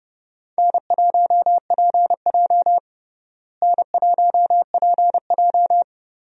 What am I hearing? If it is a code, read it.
Morse code, heard as a keyed tone, about 30 wpm, 700 Hz. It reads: N1PJ N1PJ
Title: USA Call Signs - 4 Characters - Repeated 2x - 30wpm